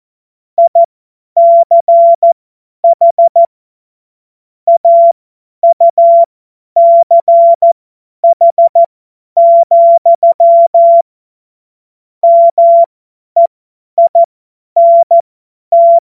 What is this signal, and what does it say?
Morse code, heard as a keyed tone, about 14 wpm, spaced as for 10 wpm, 680 Hz. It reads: ICH AUCH, MEINT
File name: Morsekurs / Witze / Witz 16 Januar 2024 50 F70